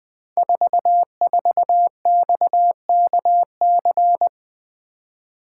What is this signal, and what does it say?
Morse code, heard as a keyed tone, about 20 wpm, 700 Hz. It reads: 44XKC
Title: Code Groups - 5 Letter:Numbers - 20wpm